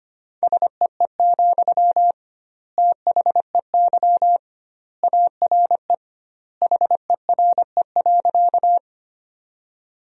Morse code, heard as a keyed tone, about 25 wpm, 700 Hz. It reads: SEE, THEY ARE HERE.
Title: 2x Word-Spacing - Sentences from Top 200 Words - 25wpm